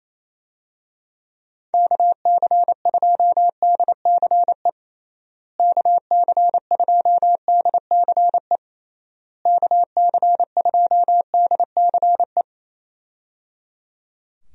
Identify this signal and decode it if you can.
Morse code, heard as a keyed tone, about 27 wpm, 700 Hz. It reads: KC2DCE KC2DCE KC2DCE
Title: USA Call Signs - 6 Characters - Repeated 3x - 28wpm